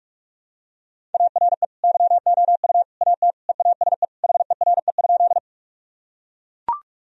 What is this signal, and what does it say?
Morse code, heard as a keyed tone, about 42 wpm, 700 Hz. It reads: ARE YOU AT EASE HERE?
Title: Sentences from Top 1000 Words - Review Part 1 - 45wpm